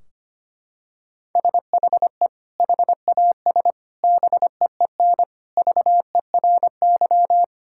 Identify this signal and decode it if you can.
Morse code, heard as a keyed tone, about 25 wpm, 700 Hz. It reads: SHE HAS BEEN VERY
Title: Sentences from Top 1600 Words 25wpm